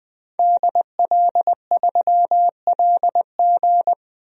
Morse code, heard as a keyed tone, about 20 wpm, 700 Hz. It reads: DL3LG
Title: DXCC-100 Call Signs - 5 Characters - 20wpm